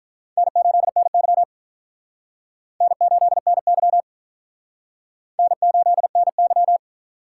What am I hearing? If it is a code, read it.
Morse code, heard as a keyed tone, about 48 wpm, 700 Hz. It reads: N8NY N8NY N8NY